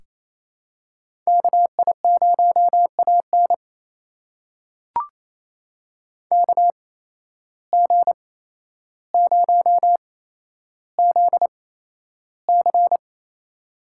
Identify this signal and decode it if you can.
Morse code, heard as a keyed tone, about 28 wpm, spaced as for 12 wpm, 700 Hz. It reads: KI0AN KG0ZC
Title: USA Call Signs - 5 Characters - 8x Character Spacing - 28wpm